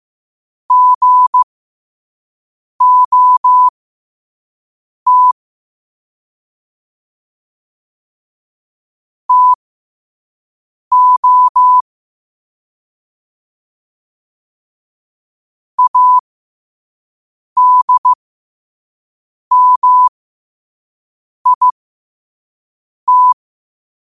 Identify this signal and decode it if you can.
Morse code, heard as a keyed tone, about 15 wpm, spaced as for 5 wpm, 1000 Hz. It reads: GOT TO ADMIT
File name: qotdfiles / 5wpm